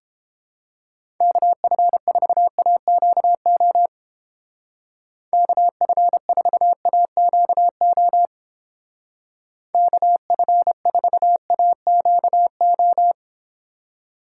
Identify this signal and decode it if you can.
Morse code, heard as a keyed tone, about 31 wpm, 700 Hz. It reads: KF4AQO KF4AQO KF4AQO